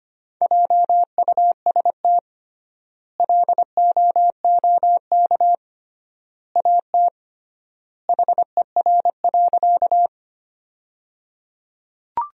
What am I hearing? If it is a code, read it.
Morse code, heard as a keyed tone, about 25 wpm, 700 Hz. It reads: JUST LOOK AT HER.
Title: 3x Word-Spacing - Sentences from Top 200 Words - 25wpm